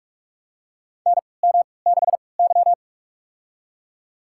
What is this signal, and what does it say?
Morse code, heard as a keyed tone, about 45 wpm, spaced as for 25 wpm, 700 Hz. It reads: NM6Y
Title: USA Call Signs - 4 Characters - 3x Character Spacing - 45wpm